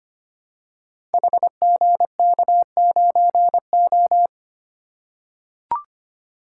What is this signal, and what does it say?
Morse code, heard as a keyed tone, about 25 wpm, 700 Hz. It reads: HGK9O
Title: Code Groups - 5 Letter:Numbers - 25wpm